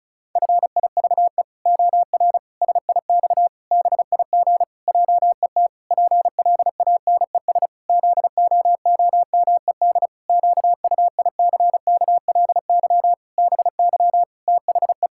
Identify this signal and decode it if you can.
Morse code, heard as a keyed tone, about 36 wpm, 700 Hz. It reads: FIVE OR SIX BIG JET PLANES ZOOMED QUICKLY BY THE
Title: Pangram Challenge v2 - 35wpm